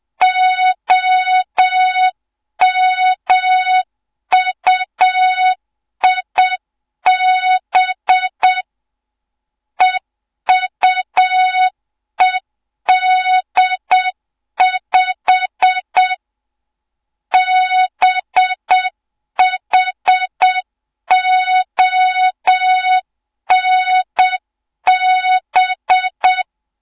Morse code, heard as a keyed tone, about 7 wpm, 3700 Hz. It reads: OMUIB EUED5 BHONB
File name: Morze-HA2MN / hang / 07 GYAKORLAT